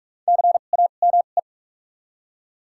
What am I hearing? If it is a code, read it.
Morse code, heard as a keyed tone, about 42 wpm, spaced as for 31 wpm, 700 Hz. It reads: CAME
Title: 2x Character Spacing - Top 200 Words - 45wpm